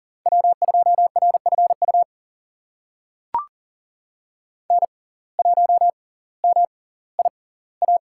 Morse code, heard as a keyed tone, about 40 wpm, spaced as for 20 wpm, 700 Hz. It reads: W2RFU N1MIA
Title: USA Call Signs - 5 Characters - 6x Character Spacing - 40wpm